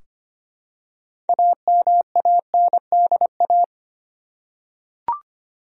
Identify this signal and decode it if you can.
Morse code, heard as a keyed tone, about 25 wpm, 700 Hz. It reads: AMANDA
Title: USA Common Names - 25wpm